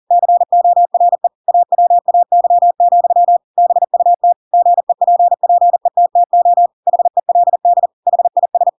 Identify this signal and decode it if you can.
Morse code, heard as a keyed tone, about 40 wpm, 700 Hz. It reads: CORE AWAY, BUT GEPPETTO HELD HIS
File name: Pinocchio / Pinocchio ch7 40 40 ik1hge